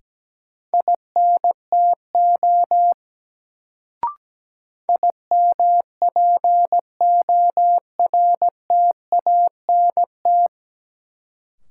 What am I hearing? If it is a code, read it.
Morse code, heard as a keyed tone, about 17 wpm, 700 Hz. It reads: INTO IMPORTANT
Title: Sentences from Top 600 Words - New Word Drill - 17wpm